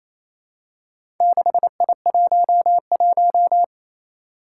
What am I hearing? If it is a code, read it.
Morse code, heard as a keyed tone, about 28 wpm, 700 Hz. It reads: 6I11
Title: Four Challenging Letter-Number Combinations 28wpm